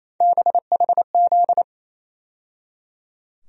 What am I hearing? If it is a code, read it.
Morse code, heard as a keyed tone, about 28 wpm, 700 Hz. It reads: BHZ
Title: USA Call Sign Postfixes - 3 Letters 28wpm